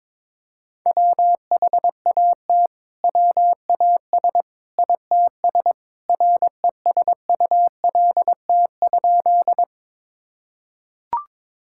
Morse code, heard as a keyed tone, about 22 wpm, 700 Hz. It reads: WHAT WAS ITS RESULT?
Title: Sentences from Top 1000 Words - Review Part 5 - 22wpm